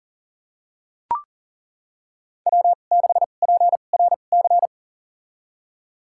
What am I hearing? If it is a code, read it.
Morse code, heard as a keyed tone, about 40 wpm, spaced as for 29 wpm, 700 Hz. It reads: W6PRC
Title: USA Call Signs - 5 Characters - 2x Character Spacing - 40wpm